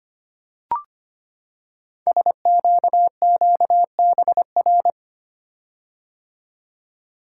Morse code, heard as a keyed tone, about 25 wpm, 700 Hz. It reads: SQQBR
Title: Code Groups - 5 Letters - 25wpm